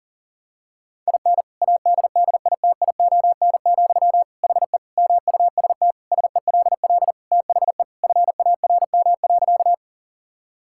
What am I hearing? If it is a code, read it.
Morse code, heard as a keyed tone, about 40 wpm, 700 Hz. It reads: IN ADDITION, HE MUST SELL THE FARM.
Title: Sentences from Top 1000 Words - Review Part 6 - 40wpm